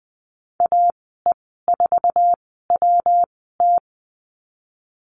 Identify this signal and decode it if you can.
Morse code, heard as a keyed tone, about 20 wpm, spaced as for 15 wpm, 700 Hz. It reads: AE4WT
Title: 154 Lesson 25g - Letter p - Call Signs 20wpm